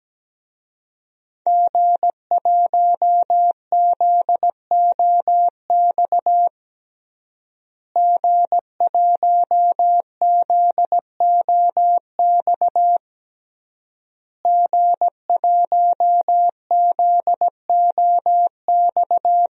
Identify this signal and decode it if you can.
Morse code, heard as a keyed tone, about 17 wpm, 700 Hz. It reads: G1ZOX G1ZOX G1ZOX
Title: DXCC-100 Call Signs - 5 Characters - Repeated 3x - 17wpm